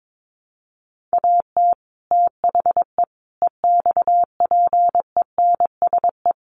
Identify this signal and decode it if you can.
Morse code, heard as a keyed tone, about 22 wpm, 700 Hz. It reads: AT THE EXPENSE
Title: Sets of 4 Words 22wpm